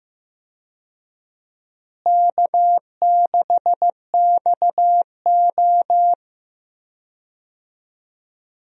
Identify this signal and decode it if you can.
Morse code, heard as a keyed tone, about 15 wpm, 700 Hz. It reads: K6XO
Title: USA Call Signs - 4 Characters - Repeated 3x - 15wpm